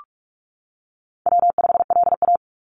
Morse code, heard as a keyed tone, about 46 wpm, 700 Hz. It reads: W5LA
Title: Call Signs - 45wpm